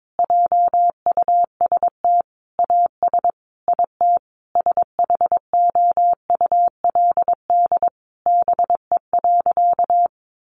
Morse code, heard as a keyed tone, about 22 wpm, 700 Hz. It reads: JUST AS IT SHOULD BE.
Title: Sentences from Top 700 Words 22wpm